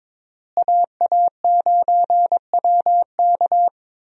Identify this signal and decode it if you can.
Morse code, heard as a keyed tone, about 22 wpm, 700 Hz. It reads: AA9WK
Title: USA Call Signs - 5 Characters - Repeated 3x - 22wpm